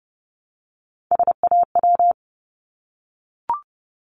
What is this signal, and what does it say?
Morse code, heard as a keyed tone, about 30 wpm, 700 Hz. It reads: SAW
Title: Multi-letter Phonemes Repeated 3x 30wpm